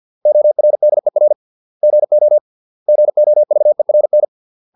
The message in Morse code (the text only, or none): ORDER GO GOVERN